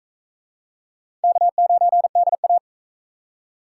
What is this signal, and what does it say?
Morse code, heard as a keyed tone, about 38 wpm, 700 Hz. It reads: K9DA